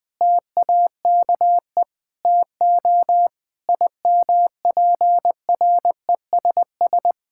Morse code, heard as a keyed tone, about 20 wpm, 700 Hz. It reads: TAKE TO IMPRESS
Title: Sentences from Top 1600 Words 20wpm